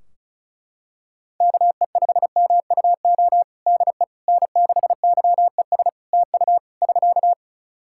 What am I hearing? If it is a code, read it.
Morse code, heard as a keyed tone, about 36 wpm, 700 Hz. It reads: KE5MUO DE N6YES TU <SK>
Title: K1USN SST QSO Practice 35wpm